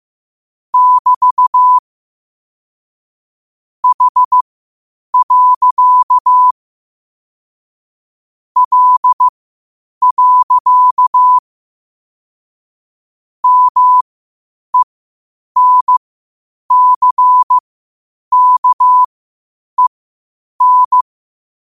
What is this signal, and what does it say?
Morse code, heard as a keyed tone, about 15 wpm, spaced as for 8 wpm, 1000 Hz. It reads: <BT> H. L. MENCKEN
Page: Quotes for Fri, 15 Aug 2025 in Morse Code at 8 words per minute.